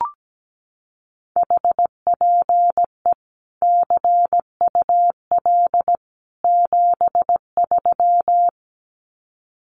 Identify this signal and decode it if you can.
Morse code, heard as a keyed tone, about 17 wpm, 700 Hz. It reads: HPE CUL 73
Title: Common QSO Phrases 17wpm